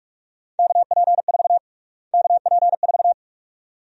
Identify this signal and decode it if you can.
Morse code, heard as a keyed tone, about 46 wpm, 700 Hz. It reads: KP4 KP4
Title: USA Call Sign Prefixes - Repeated 2x - 45wpm